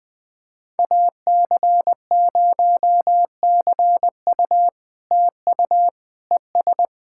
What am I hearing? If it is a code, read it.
Morse code, heard as a keyed tone, about 20 wpm, 700 Hz. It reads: AC0CU TU ES
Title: POTA QSO Practice - 20wpm